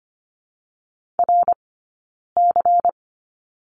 Morse code, heard as a keyed tone, about 25 wpm, 700 Hz. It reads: R C
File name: course / 25:10wpm / 092 Lesson 16d - Letter c - Two Characters 25wpm